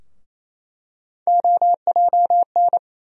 Morse code, heard as a keyed tone, about 28 wpm, 700 Hz. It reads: OJN
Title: USA Call Sign Postfixes - 3 Letters 28wpm